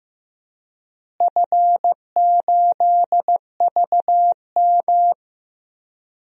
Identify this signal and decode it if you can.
Morse code, heard as a keyed tone, about 15 wpm, 700 Hz. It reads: F8VM
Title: DXCC-100 Call Signs - 4 Characters - Repeated 3x - 15wpm